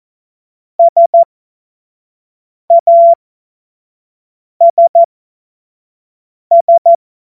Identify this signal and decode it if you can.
Morse code, heard as a keyed tone, about 14 wpm, spaced as for 5 wpm, 680 Hz. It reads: SASS
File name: Morsekurs / Witze / Witz 20. Juni 2023 25 F70